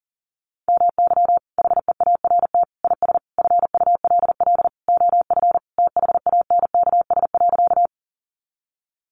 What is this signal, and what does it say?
Morse code, heard as a keyed tone, about 40 wpm, 700 Hz. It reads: MY HEART IS FULL OF THANKS.
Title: Sentences from Top 1200 Words 40wpm